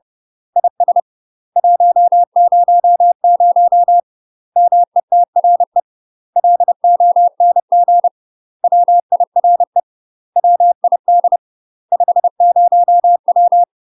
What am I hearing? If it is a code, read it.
Morse code, heard as a keyed tone, about 30 wpm, 700 Hz. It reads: IS 100 METRE LONG WIRE WID 50W
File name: UK MASTER QSOs TRAINING / 30WPM / UK MASTER QSOs 0017